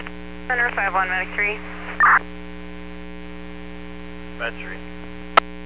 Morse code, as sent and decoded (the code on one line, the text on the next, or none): none